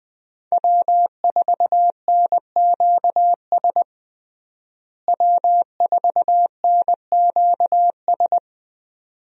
Morse code, heard as a keyed tone, about 20 wpm, 700 Hz. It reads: W4NQS W4NQS
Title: USA Call Signs - 5 Characters - Repeated 2x - 20wpm